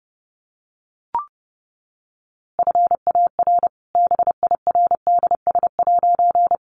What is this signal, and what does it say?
Morse code, heard as a keyed tone, about 30 wpm, 700 Hz. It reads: FAR BIRDS'
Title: A Sound Of Thunder 30wpm